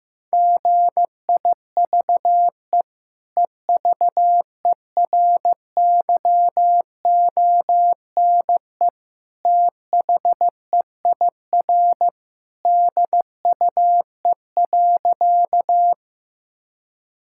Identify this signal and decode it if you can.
Morse code, heard as a keyed tone, about 15 wpm, 700 Hz. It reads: GIVE EVERYONE THEIR DUE.
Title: Sentences from Top 1000 Words - Review Part 6 - 15wpm